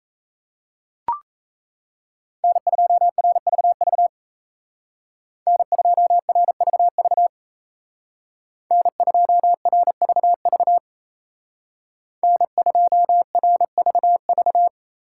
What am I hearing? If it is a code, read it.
Morse code, heard as a keyed tone, about 34 wpm, 700 Hz. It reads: N2RVV N2RVV N2RVV N2RVV
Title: Speed Racing - USA 1x3 Call Signs 28wpm